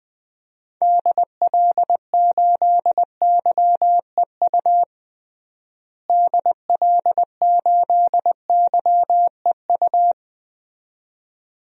Morse code, heard as a keyed tone, about 20 wpm, 700 Hz. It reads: DL8YEU DL8YEU
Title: DXCC-100 Call Signs - 6 Characters - Repeated 2x - 20wpm